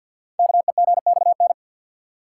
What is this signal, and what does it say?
Morse code, heard as a keyed tone, about 51 wpm, 700 Hz. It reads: KEGXN
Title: Code Groups - 5 Letters - 50wpm